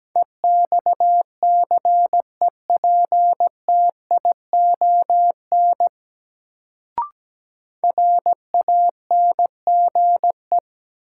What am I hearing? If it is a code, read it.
Morse code, heard as a keyed tone, about 17 wpm, 700 Hz. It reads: EXCEPTION RANGE
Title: Sets of 4 Words - New Word Drill - 17wpm